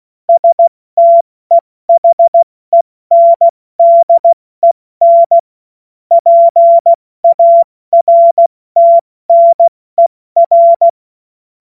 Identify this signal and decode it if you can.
Morse code, heard as a keyed tone, about 16 wpm, spaced as for 14 wpm, 680 Hz. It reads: STEHENDEN PARTNER
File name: Morsekurs / Witze / Witz 19 Dezember 2023 70 F80